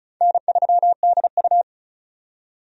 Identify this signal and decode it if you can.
Morse code, heard as a keyed tone, about 36 wpm, 700 Hz. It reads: N3DU
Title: USA Call Signs - 4 Characters - 35wpm